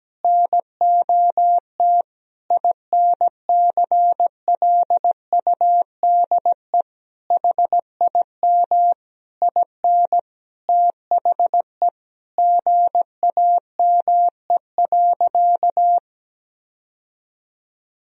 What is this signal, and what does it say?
Morse code, heard as a keyed tone, about 17 wpm, 700 Hz. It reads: NOT INCLUDE HIM IN THE GAME.
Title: Sentences from Top 1000 Words - Review Part 3 - 17wpm